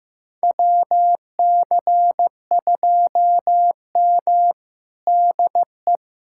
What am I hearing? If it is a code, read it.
Morse code, heard as a keyed tone, about 15 wpm, 700 Hz. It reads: WC2M DE
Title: USA Callsign DE Callsign - 4 Characters - 15wpm